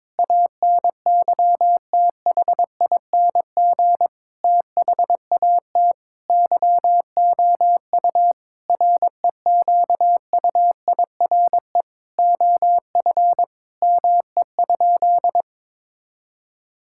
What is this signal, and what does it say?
Morse code, heard as a keyed tone, about 22 wpm, 700 Hz. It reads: ANYTHING THAT YOU REQUIRE OF ME?
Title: Sentences from Top 1000 Words - Review Part 6 - 22wpm